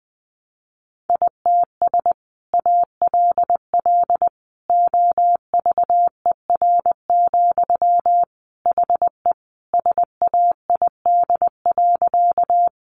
Morse code, heard as a keyed tone, about 20 wpm, 700 Hz. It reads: ITS ALL OVER, HE SAID.
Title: Sentences from Top 600 Words 20wpm